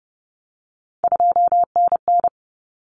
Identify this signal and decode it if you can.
Morse code, heard as a keyed tone, about 30 wpm, 700 Hz. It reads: 2NN
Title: Common Cut Numbers - 3-Digit 30wpm